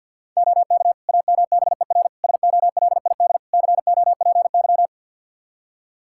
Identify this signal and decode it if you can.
Morse code, heard as a keyed tone, about 52 wpm, 700 Hz. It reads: OK AMBER SOLID COPY
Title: Practice QSO - Exchange 3 - More Info - 50wpm